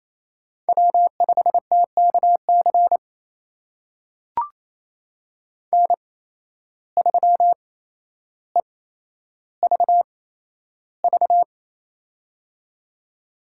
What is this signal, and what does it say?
Morse code, heard as a keyed tone, about 28 wpm, spaced as for 12 wpm, 700 Hz. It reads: W5TKC N3EVV
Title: USA Call Signs - 5 Characters - 8x Character Spacing - 28wpm